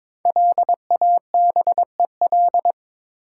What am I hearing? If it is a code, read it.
Morse code, heard as a keyed tone, about 22 wpm, 700 Hz. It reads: LABEL
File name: 5 Letter Words - First 3 Letters Given - 22wpm